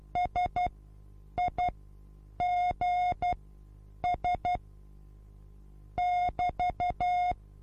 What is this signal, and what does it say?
Morse code, heard as a keyed tone, about 12 wpm, spaced as for 8 wpm, 720 Hz. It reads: SIGS <BT>
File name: Onlinelehrgang / morsen / DJ4UF Morsen23